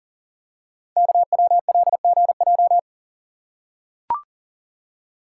Morse code, heard as a keyed tone, about 39 wpm, 700 Hz. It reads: KWLGJ
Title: Code Groups - 5 Letters - 40wpm